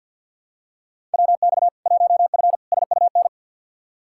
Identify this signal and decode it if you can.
Morse code, heard as a keyed tone, about 50 wpm, 700 Hz. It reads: WX 1F SUN
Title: Practice QSO - Exchange 2 - Info - 50wpm